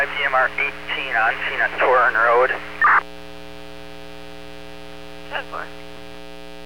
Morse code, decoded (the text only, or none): none